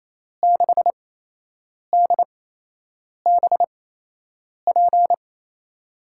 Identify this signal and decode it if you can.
Morse code, heard as a keyed tone, about 28 wpm, spaced as for 8 wpm, 700 Hz. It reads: 6DBP
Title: USA Call Signs - 6 Characters - 8x Character Spacing - 28wpm